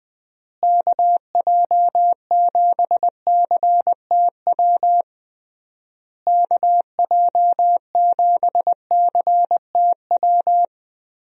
Morse code, heard as a keyed tone, about 20 wpm, 700 Hz. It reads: KJ7CTW KJ7CTW
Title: USA Call Signs - 6 Characters - Repeated 3x - 20wpm